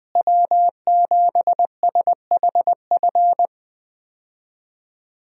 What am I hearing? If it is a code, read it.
Morse code, heard as a keyed tone, about 20 wpm, 700 Hz. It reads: W7SHF
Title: USA Call Signs - 5 Characters - Repeated 2x - 20wpm